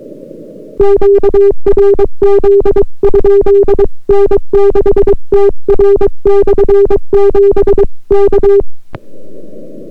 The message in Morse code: QRZ? N6TR/7 K